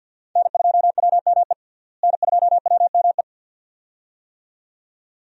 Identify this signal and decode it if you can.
Morse code, heard as a keyed tone, about 52 wpm, 700 Hz. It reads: N2WME N2WME
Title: USA Call Signs - 5 Characters - Repeated 2x - 50wpm